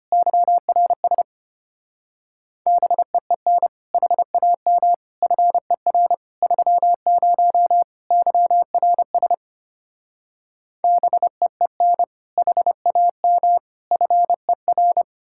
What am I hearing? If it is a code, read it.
Morse code, heard as a keyed tone, about 33 wpm, 700 Hz. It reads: YRS BEEN HAM FER 30 YRS BEEN HAM FER